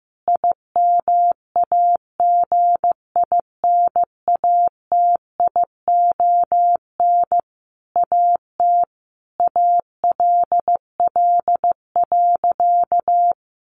Morse code, heard as a keyed tone, about 15 wpm, 700 Hz. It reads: IMAGINATION AT ALL.
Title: Sentences from Top 1300 Words 15wpm